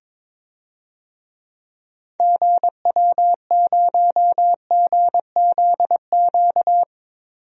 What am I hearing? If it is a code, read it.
Morse code, heard as a keyed tone, about 22 wpm, 700 Hz. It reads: GW0GZQ
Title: DXCC-100 Call Signs - 6 Characters - 22wpm